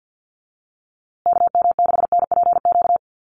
Morse code, heard as a keyed tone, about 50 wpm, 700 Hz. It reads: KM6NPQ
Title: USA Call Signs Repeated 3x 50wpm